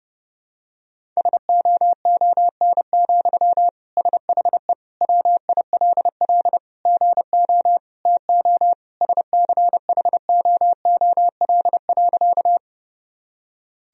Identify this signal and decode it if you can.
Morse code, heard as a keyed tone, about 30 wpm, 700 Hz. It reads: SOON, SHE WILL GO TO SCHOOL.
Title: Sentences from Top 1000 Words - Review Part 1 - 30wpm